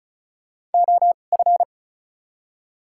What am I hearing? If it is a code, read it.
Morse code, heard as a keyed tone, about 34 wpm, spaced as for 25 wpm, 700 Hz. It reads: OF
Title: 2x Character Spacing - Top 300 Words - 35wpm